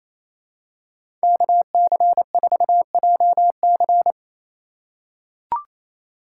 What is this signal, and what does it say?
Morse code, heard as a keyed tone, about 28 wpm, 700 Hz. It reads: KC4JC
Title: USA Call Signs - 5 Characters - 2x Character Spacing - 28wpm